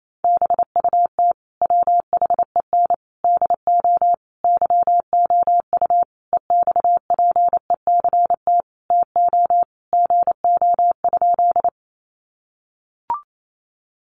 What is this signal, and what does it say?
Morse code, heard as a keyed tone, about 28 wpm, 700 Hz. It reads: BUT WHEN DO YOU EXPECT TO GO?
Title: Sentences from Top 1100 Words 28wpm